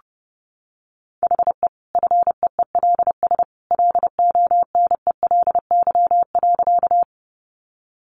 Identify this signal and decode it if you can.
Morse code, heard as a keyed tone, about 30 wpm, 700 Hz. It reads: HE FEELS LONELY.
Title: Easy Phrases 30wpm